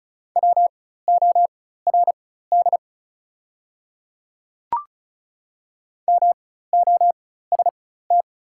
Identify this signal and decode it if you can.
Morse code, heard as a keyed tone, about 36 wpm, spaced as for 16 wpm, 700 Hz. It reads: WORD MOST
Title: 4x Character Spacing - Top 100 Words - 35wpm